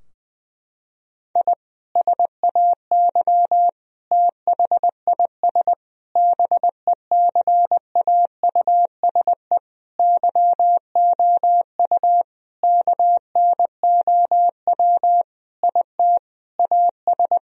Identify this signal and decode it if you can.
Morse code, heard as a keyed tone, about 20 wpm, 700 Hz. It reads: I SAY THIS BECAUSE YOU KNOW IT AS
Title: Sentences from Top 1000 Words - Review Part 3 - 20wpm